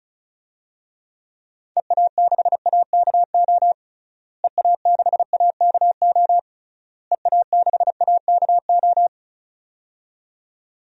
Morse code, heard as a keyed tone, about 36 wpm, 700 Hz. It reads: EA6AKO EA6AKO EA6AKO
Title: DXCC-100 Call Signs - 6 Characters - Repeated 3x - 35wpm